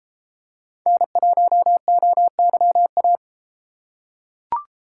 Morse code, heard as a keyed tone, about 32 wpm, 700 Hz. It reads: N1OYA